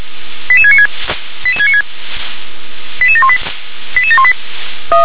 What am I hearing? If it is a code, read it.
Five-tone selective call (ZVEI), heard as quick runs of five tones, tones of about 70 ms. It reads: 88696 88696 88718 88718